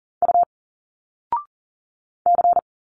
Morse code, heard as a keyed tone, about 40 wpm, 700 Hz. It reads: U C
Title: Single Letters 40wpm